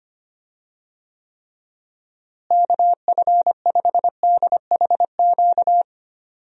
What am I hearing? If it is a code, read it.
Morse code, heard as a keyed tone, about 25 wpm, 700 Hz. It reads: KF5DHQ
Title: USA Call Signs - 6 Characters - Repeated 3x - 25wpm